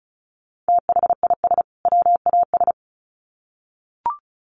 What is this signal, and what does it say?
Morse code, heard as a keyed tone, about 34 wpm, 700 Hz. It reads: THIS WAS
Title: Sets of 2 Words 35wpm